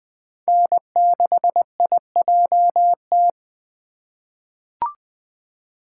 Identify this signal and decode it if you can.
Morse code, heard as a keyed tone, about 20 wpm, 700 Hz. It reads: N6IJT
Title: USA Call Signs - 5 Characters - 2x Character Spacing - 20wpm